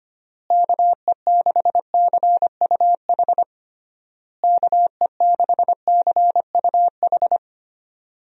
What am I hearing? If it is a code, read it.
Morse code, heard as a keyed tone, about 25 wpm, 700 Hz. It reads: KE6CUH KE6CUH
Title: USA Call Signs - 6 Characters - Repeated 3x - 25wpm